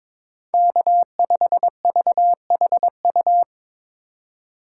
Morse code, heard as a keyed tone, about 22 wpm, 700 Hz. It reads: K5VHU
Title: USA Call Signs - 2x Character Spacing - 22wpm